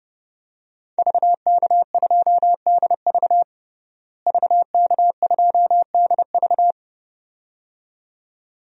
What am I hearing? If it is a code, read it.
Morse code, heard as a keyed tone, about 32 wpm, 700 Hz. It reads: VK2DV VK2DV
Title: DXCC-100 Call Signs - 5 Characters - Repeated 2x - 30wpm